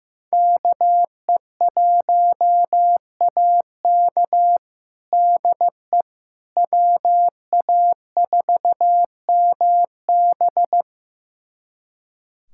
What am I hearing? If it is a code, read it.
Morse code, heard as a keyed tone, about 15 wpm, 700 Hz. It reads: KE1AK DE WA4MB
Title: USA Callsign DE Callsign - 5 Characters - 15wpm